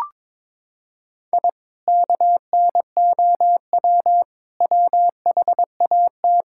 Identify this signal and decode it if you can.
Morse code, heard as a keyed tone, about 22 wpm, 700 Hz. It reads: I KNOW WHAT
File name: practice / 22wpm / Sentences with 2-Word Phrases - 22wpm